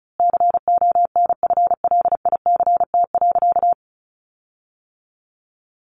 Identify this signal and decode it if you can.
Morse code, heard as a keyed tone, about 34 wpm, 700 Hz. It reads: CONFLICT.
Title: Sentences from Top 1300 Words 35wpm